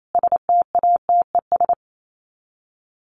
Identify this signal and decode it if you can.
Morse code, heard as a keyed tone, about 28 wpm, 700 Hz. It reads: STATES
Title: Sets of 5 Words 28wpm